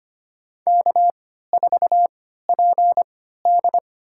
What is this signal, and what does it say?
Morse code, heard as a keyed tone, about 25 wpm, spaced as for 14 wpm, 700 Hz. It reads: K4PD
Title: USA Call Signs - 4 Characters - 3x Character Spacing - 25wpm